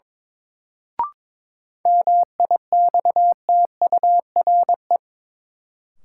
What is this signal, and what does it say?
Morse code, heard as a keyed tone, about 22 wpm, 700 Hz. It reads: MIXTURE
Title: Sentences from Top 1500 Words - New Word Drill - 22wpm